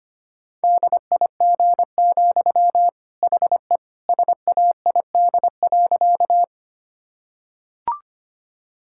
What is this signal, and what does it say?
Morse code, heard as a keyed tone, about 25 wpm, 700 Hz. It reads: DIG, HE SAID.
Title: Sentences from Top 1600 Words 25wpm